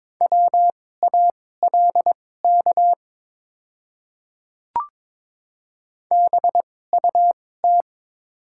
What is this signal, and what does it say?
Morse code, heard as a keyed tone, about 22 wpm, spaced as for 16 wpm, 700 Hz. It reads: WALK BUT
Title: 2x Character Spacing - Top 300 Words - 22wpm